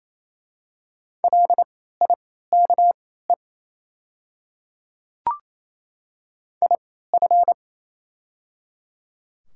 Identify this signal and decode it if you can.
Morse code, heard as a keyed tone, about 28 wpm, spaced as for 16 wpm, 700 Hz. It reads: LIKE IF
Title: 3x Character Spacing - Top 100 Words - 28wpm